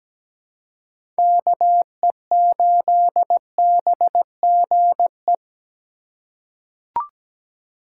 Morse code, heard as a keyed tone, about 17 wpm, 700 Hz. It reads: KE8BGE
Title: USA Call Signs - 6 Characters - 4x Character Spacing - 17wpm